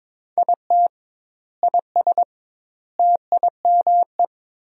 Morse code, heard as a keyed tone, about 22 wpm, 700 Hz. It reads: IT IS TIME
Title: 2x Word-Spacing - Sentences from Top 500 Words - 22wpm